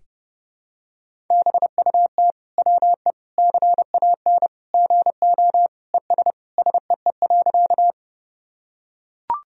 Morse code, heard as a keyed tone, about 32 wpm, 700 Hz. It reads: BUT WE CAN GO ES SEE.
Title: Sentences from Top 100 Words and Abbreviations 30wpm